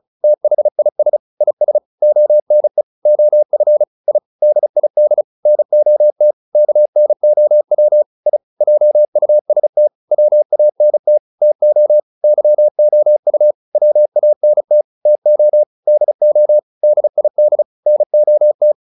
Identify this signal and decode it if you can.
Morse code, heard as a keyed tone, about 34 wpm, 600 Hz. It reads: THIS IS ONE OF I DID NOT KNOW I JUST WANT TO YOU WANT TO DO DID NOT